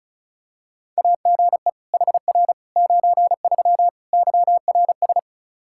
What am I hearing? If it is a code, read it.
Morse code, heard as a keyed tone, about 34 wpm, 700 Hz. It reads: AGE HR 93 YRS
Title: Practice QSO - Exchange 3 - More Info - 35wpm